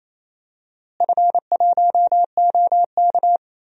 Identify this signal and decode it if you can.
Morse code, heard as a keyed tone, about 28 wpm, 700 Hz. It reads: F1OK
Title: DXCC-100 Call Signs - 4 Characters - Repeated 3x - 28wpm